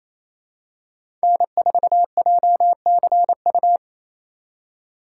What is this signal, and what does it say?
Morse code, heard as a keyed tone, about 28 wpm, 700 Hz. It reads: N4JCU
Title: USA Call Signs - 3x Character Spacing - 28wpm